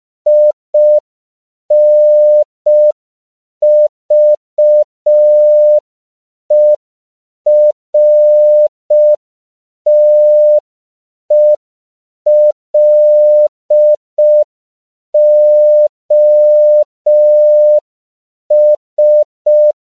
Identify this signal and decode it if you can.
Morse code, heard as a keyed tone, about 5 wpm, 600 Hz. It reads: INVERTELOS
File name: CW / 5PPM